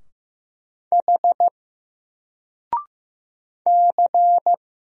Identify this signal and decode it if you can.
Morse code, heard as a keyed tone, about 15 wpm, 700 Hz. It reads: H C
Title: Single Character (Letter:Number:Punctuation:Prosign) 15wpm